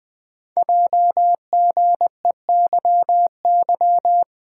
Morse code, heard as a keyed tone, about 20 wpm, 700 Hz. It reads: JGEYY
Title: Code Groups - 5 Letters - 20wpm